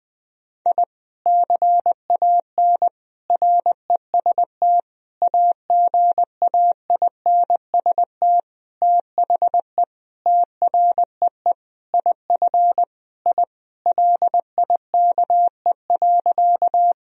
Morse code, heard as a keyed tone, about 20 wpm, 700 Hz. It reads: I CAN REST AGAINST THE TREE IF I LIKE.
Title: Sentences from Top 1000 Words - Review Part 1 - 20wpm